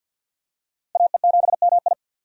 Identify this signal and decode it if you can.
Morse code, heard as a keyed tone, about 51 wpm, 700 Hz. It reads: AE7MI
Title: USA Call Signs - 5 Characters - 50wpm